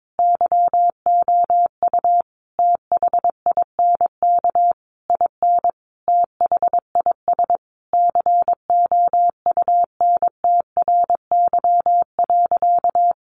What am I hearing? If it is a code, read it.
Morse code, heard as a keyed tone, about 22 wpm, 700 Hz. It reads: YOU THINK IN THIS COUNTRY.
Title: Sentences from Top 1200 Words 22wpm